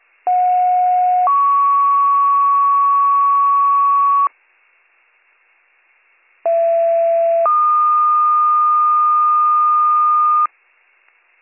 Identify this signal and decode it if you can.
Two-tone page, heard as a one-second tone, then a three-second tone, about 730 then 1120 Hz, about 670 then 1220 Hz. It is heard from 0.3 to 10.5 s.